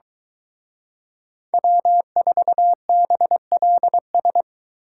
Morse code, heard as a keyed tone, about 23 wpm, 700 Hz. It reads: W4BLS